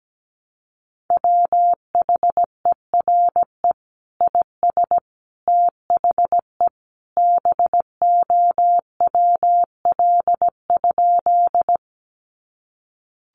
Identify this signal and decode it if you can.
Morse code, heard as a keyed tone, about 17 wpm, 700 Hz. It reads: WHERE IS THE BOWL?
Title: Sentences from Top 1200 Words 17wpm